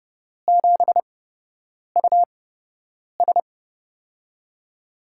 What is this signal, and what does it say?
Morse code, heard as a keyed tone, about 29 wpm, spaced as for 8 wpm, 700 Hz. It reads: 7US
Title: USA Call Signs - 4 Characters - 8x Character Spacing - 30wpm